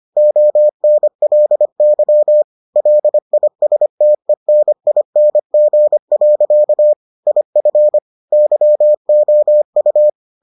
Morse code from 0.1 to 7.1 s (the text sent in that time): ONLY LISTENING.